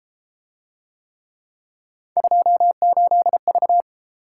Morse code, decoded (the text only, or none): 28V